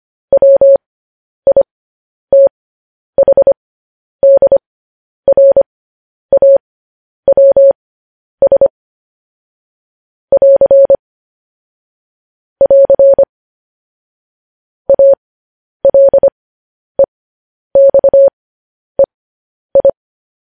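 Morse code, heard as a keyed tone, about 25 wpm, spaced as for 10 wpm, 550 Hz. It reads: WITHDRAWS <AR> <AR> ALEXEI
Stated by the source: Morse code sending of news headlines, speed 10.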